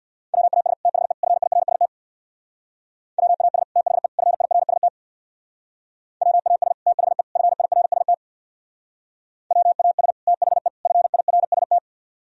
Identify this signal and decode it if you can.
Morse code, heard as a keyed tone, about 64 wpm, 700 Hz. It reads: WAS THE FIRST WAS THE FIRST WAS THE FIRST WAS THE FIRST